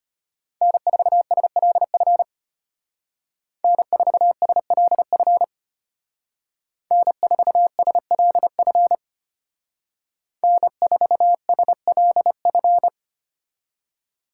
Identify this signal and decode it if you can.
Morse code, heard as a keyed tone, about 31 wpm, 700 Hz. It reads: N4SLF N4SLF N4SLF N4SLF